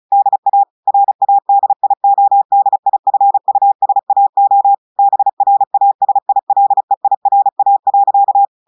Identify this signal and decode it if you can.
Morse code, heard as a keyed tone, about 35 wpm, 800 Hz. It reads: DA RADIODIFUSAO BRASILEIRA.
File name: audios / portugais / portugais texte en clair 1 vitesse 35 mots minutes